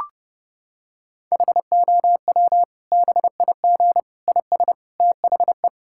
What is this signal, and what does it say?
Morse code, heard as a keyed tone, about 30 wpm, 700 Hz. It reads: HOW BIG IS THE
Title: Sentences from Top 1000 Words - Review Part 5 - 30wpm